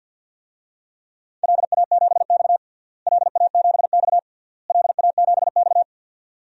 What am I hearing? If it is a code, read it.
Morse code, heard as a keyed tone, about 50 wpm, 700 Hz. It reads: LA7X LA7X LA7X
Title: DXCC-100 Call Signs - 4 Characters - Repeated 3x - 50wpm